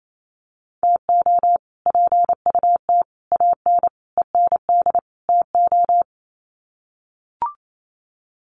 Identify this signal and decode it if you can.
Morse code, heard as a keyed tone, about 28 wpm, 700 Hz. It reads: TO PUT AN END TO
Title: Sets of 5 Words Encore2 28wpm